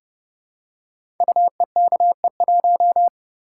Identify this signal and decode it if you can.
Morse code, heard as a keyed tone, about 30 wpm, 700 Hz. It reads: UEKE1
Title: Code Groups - 5 Letter:Numbers - 30wpm